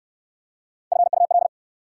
Morse code, heard as a keyed tone, about 69 wpm, 690 Hz. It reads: FUL